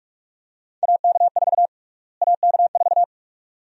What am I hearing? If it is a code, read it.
Morse code, heard as a keyed tone, about 47 wpm, 700 Hz. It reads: AK4 AK4
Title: USA Call Sign Prefixes - Repeated 2x - 45wpm